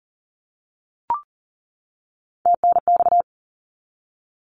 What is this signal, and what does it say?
Morse code, heard as a keyed tone, about 40 wpm, 700 Hz. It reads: TNX
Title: Abbreviations 40wpm